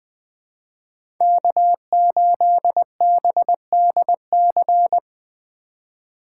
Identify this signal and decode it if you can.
Morse code, heard as a keyed tone, about 20 wpm, 700 Hz. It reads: K8BDC
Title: USA Call Signs - 5 Characters - 4x Character Spacing - 20wpm